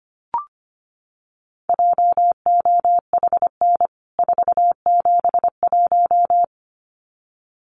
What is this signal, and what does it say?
Morse code, heard as a keyed tone, about 25 wpm, 700 Hz. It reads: JOHN 471
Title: CWT Exchange 25wpm